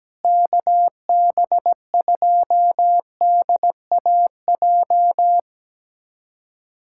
Morse code, heard as a keyed tone, about 17 wpm, 700 Hz. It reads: KB2DAJ
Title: USA Callsign DE Callsign - 6 Characters - 17wpm